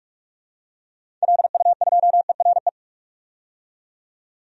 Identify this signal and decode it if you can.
Morse code, heard as a keyed tone, about 46 wpm, 700 Hz. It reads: LU2ERE
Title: DXCC-100 Call Signs - 6 Characters - 45wpm